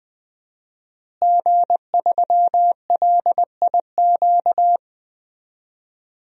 Morse code, heard as a keyed tone, about 20 wpm, 700 Hz. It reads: G3LIQ
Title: DXCC-100 Call Signs - 5 Characters - Repeated 3x - 20wpm